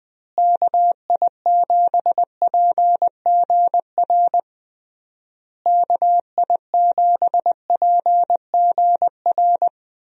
Morse code, heard as a keyed tone, about 20 wpm, 700 Hz. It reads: KI7PGR KI7PGR
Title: USA Call Signs - 6 Characters - Repeated 3x - 20wpm